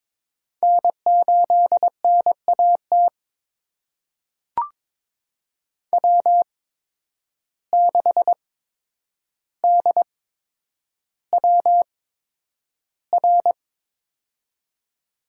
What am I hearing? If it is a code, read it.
Morse code, heard as a keyed tone, about 22 wpm, spaced as for 9 wpm, 700 Hz. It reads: N8NAT W6DWR
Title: USA Call Signs - 5 Characters - 8x Character Spacing - 22wpm